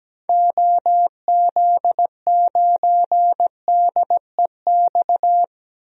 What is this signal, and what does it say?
Morse code, heard as a keyed tone, about 17 wpm, 700 Hz. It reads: OZ9DEX
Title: DXCC-100 Call Signs - 6 Characters - 17wpm